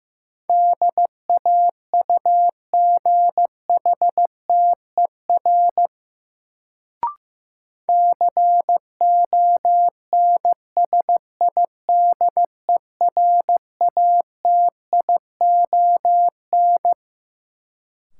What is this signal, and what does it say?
Morse code, heard as a keyed tone, about 15 wpm, 700 Hz. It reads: DAUGHTER CONSIDERATION
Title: Sentences from Top 1500 Words - New Word Drill Alt - 15wpm